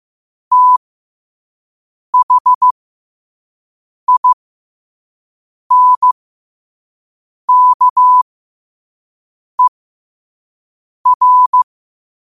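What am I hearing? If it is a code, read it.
Morse code, heard as a keyed tone, about 15 wpm, spaced as for 5 wpm, 1000 Hz. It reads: THINKER
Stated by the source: Quote-of-the-day Morse code practice at 5 wpm.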